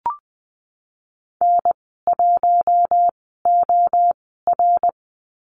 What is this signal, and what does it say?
Morse code, heard as a keyed tone, about 20 wpm, spaced as for 15 wpm, 700 Hz. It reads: N1OR
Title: 062 Lesson 11g - Letter h - Call Signs 20wpm